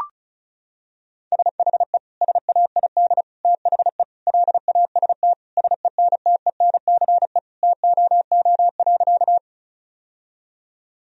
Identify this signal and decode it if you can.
Morse code, heard as a keyed tone, about 34 wpm, 700 Hz. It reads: SHE SAID THE LAST SENTENCE TOO.
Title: Sentences from Top 1000 Words - Review Part 1 - 35wpm